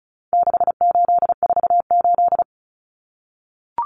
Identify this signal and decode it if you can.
Morse code, heard as a keyed tone, about 34 wpm, 700 Hz. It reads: 6848
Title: Numbers - 4-Digit 35wpm